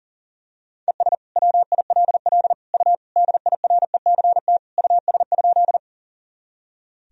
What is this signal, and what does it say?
Morse code, heard as a keyed tone, about 40 wpm, 700 Hz. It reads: ES WILL U DIRECT US?
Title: Sentences from Top 400 Words and Abbreviations 40wpm